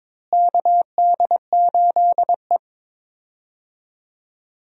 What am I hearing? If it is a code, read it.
Morse code, heard as a keyed tone, about 22 wpm, 700 Hz. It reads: KD8E
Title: USA Call Signs - 4 Characters - Repeated 2x - 22wpm